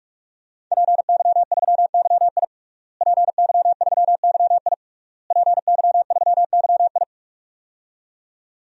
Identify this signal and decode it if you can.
Morse code, heard as a keyed tone, about 46 wpm, 700 Hz. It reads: PY3YI PY3YI PY3YI
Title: DXCC-100 Call Signs - 5 Characters - Repeated 3x - 45wpm